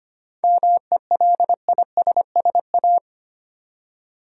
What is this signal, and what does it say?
Morse code, heard as a keyed tone, about 25 wpm, 700 Hz. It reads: MELISSA
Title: USA Common Names - 25wpm